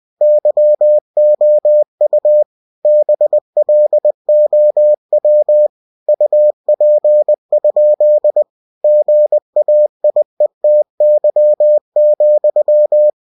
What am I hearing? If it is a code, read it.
Morse code, heard as a keyed tone, about 20 wpm, 600 Hz. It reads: YOU BLOW UP? GAIETY,
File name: cw-books / winnie-the-pooh / winnie-20wpm-page-0055